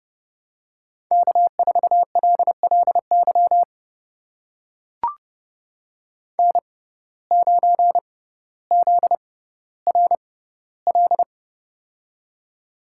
Morse code, heard as a keyed tone, about 30 wpm, spaced as for 15 wpm, 700 Hz. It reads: K4LLY N9ZRL
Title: USA Call Signs - 6x Character Spacing - 30wpm